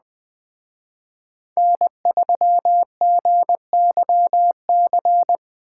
Morse code, heard as a keyed tone, about 20 wpm, 700 Hz. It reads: N3GYC
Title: Speed Racing - USA 1x3 Call Signs 17wpm